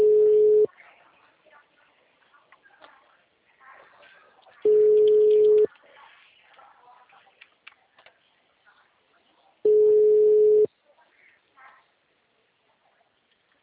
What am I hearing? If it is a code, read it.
Ringback tone (Europe, 425 Hz).